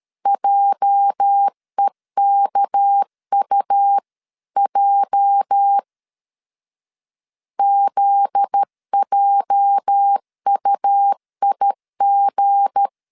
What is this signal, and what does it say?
Morse code, heard as a keyed tone, about 13 wpm, 800 Hz. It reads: JEKUJ ZJUIG
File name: LEZIONI / 8 YJ45 / 8 YJ 12wpm 2235